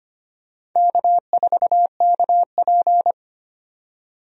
Morse code, heard as a keyed tone, about 25 wpm, 700 Hz. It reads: K4KP
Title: USA Call Signs - 4 Characters - 25wpm